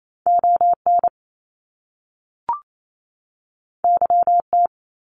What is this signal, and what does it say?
Morse code, heard as a keyed tone, about 28 wpm, 700 Hz. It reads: ON YT
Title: Canadian Provinces and Territories 28wpm